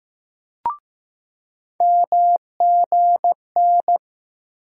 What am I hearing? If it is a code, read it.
Morse code, heard as a keyed tone, about 15 wpm, 700 Hz. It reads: MGN
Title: USA Call Sign Postfixes - 3 Letters 15wpm